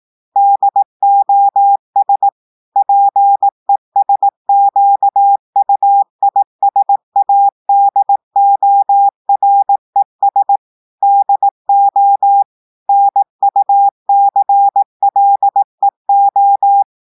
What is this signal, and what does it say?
Morse code, heard as a keyed tone, about 18 wpm, 800 Hz. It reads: DOS PESQUISADORES DO NUCLEO
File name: audios / portugais / portugais texte en clair 1 vitesse 18 mots minutes